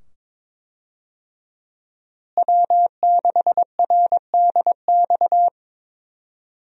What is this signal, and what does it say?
Morse code, heard as a keyed tone, about 22 wpm, 700 Hz. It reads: W6RDX
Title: USA Call Signs - 5 Characters - Repeated 2x - 22wpm